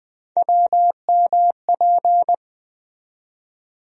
Morse code, heard as a keyed tone, about 20 wpm, 700 Hz. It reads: WMP
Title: USA Call Sign Postfixes - 3 Letters 20wpm